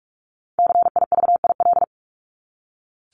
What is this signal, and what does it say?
Morse code, heard as a keyed tone, about 44 wpm, 700 Hz. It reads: CIVIL
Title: Top 1751-2000 Words 45wpm